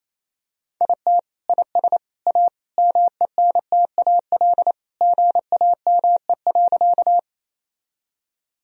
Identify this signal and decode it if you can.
Morse code, heard as a keyed tone, about 28 wpm, 700 Hz. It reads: IT IS A MENTAL GAME.
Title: Sentences from Top 1000 Words - Review Part 4 - 28wpm